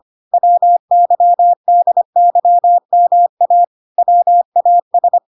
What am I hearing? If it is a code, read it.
Morse code, heard as a keyed tone, about 25 wpm, 700 Hz. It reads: WYDYMA WAS